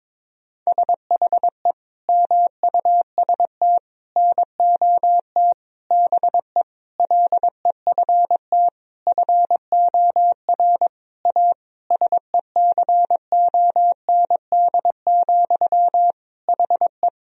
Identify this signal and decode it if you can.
Morse code, heard as a keyed tone, about 22 wpm, 700 Hz. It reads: SHE MUST NOT BE LEFT FOR A SECOND, HE
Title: Sentences from Top 1000 Words - Review Part 1 - 22wpm